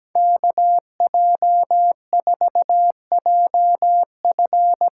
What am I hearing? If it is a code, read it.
Morse code, heard as a keyed tone, about 17 wpm, 700 Hz. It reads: KJ4JF
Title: USA Call Signs - 8x Character Spacing - 17wpm